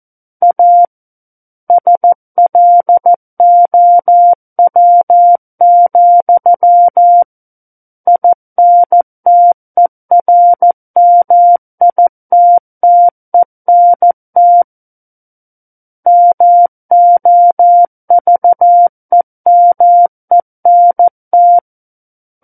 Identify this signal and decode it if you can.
Morse code, heard as a keyed tone, about 14 wpm, 700 Hz. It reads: A SLOW, INTERMITTENT MOVEMENT
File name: Book One / 05-Chapter 5 - 14 WPM